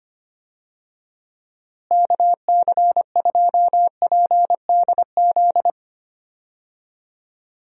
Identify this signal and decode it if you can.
Morse code, heard as a keyed tone, about 25 wpm, 700 Hz. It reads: KC2PDZ
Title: USA Call Signs - 6 Characters - Repeated 3x - 25wpm